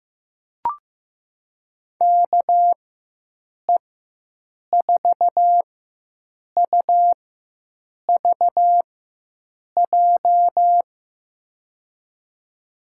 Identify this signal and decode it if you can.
Morse code, heard as a keyed tone, about 15 wpm, spaced as for 7 wpm, 700 Hz. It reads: KE4UVJ
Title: USA Call Signs - 4x Character Spacing - 15wpm